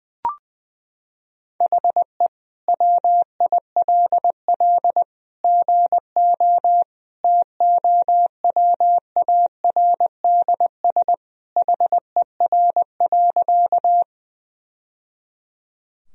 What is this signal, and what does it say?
Morse code, heard as a keyed tone, about 20 wpm, 700 Hz. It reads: HE WILL GO TOWARDS HER.
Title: Sentences from Top 1000 Words - Review Part 2 - 20wpm